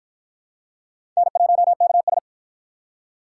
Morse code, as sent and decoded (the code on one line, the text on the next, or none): -. .---- -.- ...
N1KS